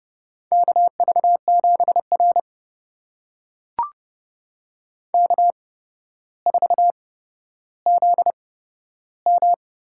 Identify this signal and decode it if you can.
Morse code, heard as a keyed tone, about 30 wpm, spaced as for 13 wpm, 700 Hz. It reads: KV7R K4ZM
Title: USA Call Signs - 4 Characters - 8x Character Spacing - 30wpm